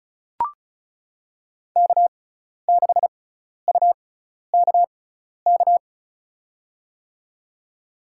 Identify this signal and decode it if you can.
Morse code, heard as a keyed tone, about 36 wpm, spaced as for 12 wpm, 700 Hz. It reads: K6UKK
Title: USA Call Signs - 6x Character Spacing - 35wpm